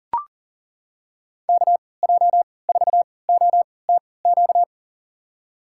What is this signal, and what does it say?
Morse code, heard as a keyed tone, about 40 wpm, spaced as for 23 wpm, 700 Hz. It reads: KJ4OTQ
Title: USA Call Signs - 6 Characters - 3x Character Spacing - 40wpm